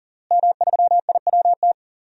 Morse code, heard as a keyed tone, about 40 wpm, 700 Hz. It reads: M3IWT
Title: DXCC-100 Call Signs - 5 Characters - Repeated 3x - 40wpm